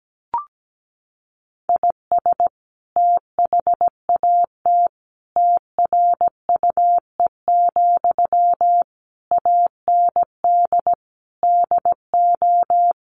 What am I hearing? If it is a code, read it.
Morse code, heard as a keyed tone, about 17 wpm, 700 Hz. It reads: IS THAT TRUE, AND DO
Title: Sentences from Top 1200 Words 17wpm